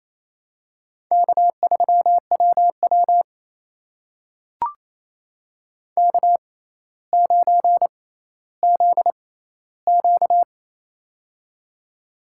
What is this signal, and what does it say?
Morse code, heard as a keyed tone, about 28 wpm, spaced as for 14 wpm, 700 Hz. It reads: K3WW K9ZQ
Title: USA Call Signs - 4 Characters - 6x Character Spacing - 28wpm